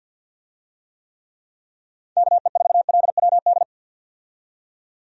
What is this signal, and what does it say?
Morse code, heard as a keyed tone, about 50 wpm, 700 Hz. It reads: KE4LWD
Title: USA Call Signs - 6 Characters - 50wpm